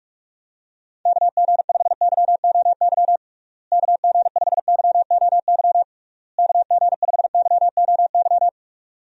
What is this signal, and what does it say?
Morse code, heard as a keyed tone, about 46 wpm, 700 Hz. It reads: KG5YOY KG5YOY KG5YOY
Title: USA Call Signs - 6 Characters - Repeated 3x - 45wpm